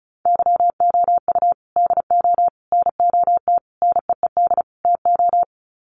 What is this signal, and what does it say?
Morse code, heard as a keyed tone, about 34 wpm, 700 Hz. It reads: YOU DO NOT NEED TO
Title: Sets of 5 Words 35wpm